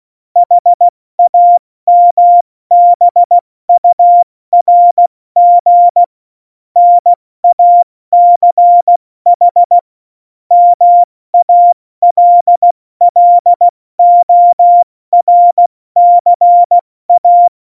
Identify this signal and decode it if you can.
Morse code, heard as a keyed tone, about 16 wpm, 700 Hz. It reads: HAMBURG NACH MALLORCA
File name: Morsekurs / Witze / Witz 19. Januar 2021, 70 F80